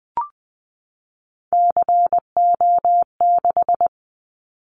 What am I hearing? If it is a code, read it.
Morse code, heard as a keyed tone, about 20 wpm, 700 Hz. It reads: CO6
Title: Callsign Prefixes for Top 100 DXCC - 20wpm